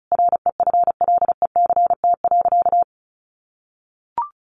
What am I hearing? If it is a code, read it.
Morse code, heard as a keyed tone, about 34 wpm, 700 Hz. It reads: REFLECT.
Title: Sentences from Top 1500 Words 35wpm